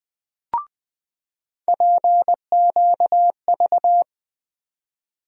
Morse code, heard as a keyed tone, about 20 wpm, 700 Hz. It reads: PQV
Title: USA Call Sign Postfixes - 3 Letters 20wpm